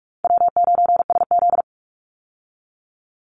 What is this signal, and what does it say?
Morse code, heard as a keyed tone, about 44 wpm, 700 Hz. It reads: W9SZ
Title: Call Signs Repeated 3x - 45wpm